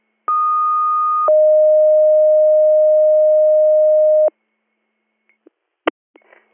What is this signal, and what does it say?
Two-tone page, heard as a one-second tone, then a three-second tone, about 1230 then 620 Hz. It is heard from 0.3 to 4.3 s.